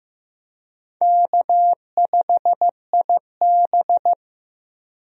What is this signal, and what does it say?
Morse code, heard as a keyed tone, about 16 wpm, 700 Hz. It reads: K5IB
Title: USA Call Signs - 4 Characters - 15wpm